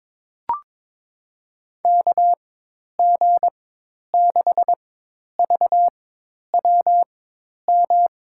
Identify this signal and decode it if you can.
Morse code, heard as a keyed tone, about 22 wpm, spaced as for 10 wpm, 700 Hz. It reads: KG6VWM
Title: USA Call Signs - 6 Characters - 4x Character Spacing - 22wpm